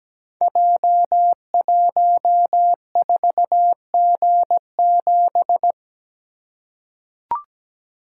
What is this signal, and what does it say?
Morse code, heard as a keyed tone, about 17 wpm, 700 Hz. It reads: J14G7
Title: Code Groups - 5 Letter:Numbers - 17wpm